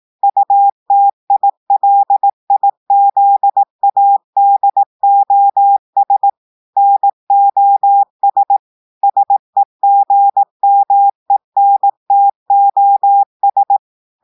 Morse code, heard as a keyed tone, about 18 wpm, 800 Hz. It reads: UTILIZADOS NOS SEGMENTOS
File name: portugais texte en clair 2 vitesse 18 mots minutes